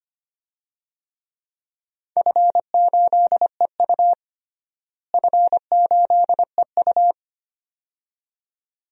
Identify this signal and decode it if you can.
Morse code, heard as a keyed tone, about 25 wpm, 700 Hz. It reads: F8EU F8EU
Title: DXCC-100 Call Signs - 4 Characters - Repeated 2x - 25wpm